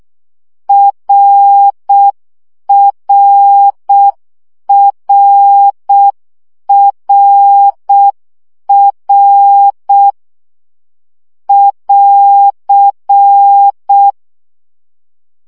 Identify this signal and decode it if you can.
Morse code, heard as a keyed tone, about 6 wpm, 800 Hz. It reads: RRRRR <AR>
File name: CW gyakorlas r 5 baud